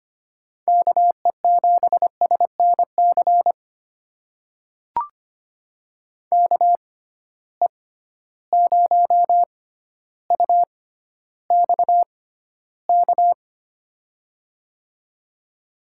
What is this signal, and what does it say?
Morse code, heard as a keyed tone, about 25 wpm, spaced as for 13 wpm, 700 Hz. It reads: KE7SNC KE0UXK
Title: USA Call Signs - 6 Characters - 6x Character Spacing - 25wpm